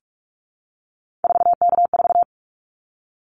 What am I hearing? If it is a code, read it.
Morse code, heard as a keyed tone, about 44 wpm, 700 Hz. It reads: 4K4
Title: Callsign Prefixes for Top 100 DXCC - 45wpm